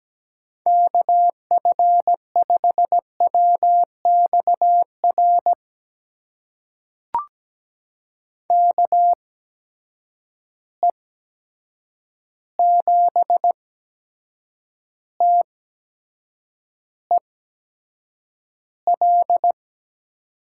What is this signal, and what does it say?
Morse code, heard as a keyed tone, about 17 wpm, spaced as for 7 wpm, 700 Hz. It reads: KF5WXR KE7TEL
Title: USA Call Signs - 6 Characters - 8x Character Spacing - 17wpm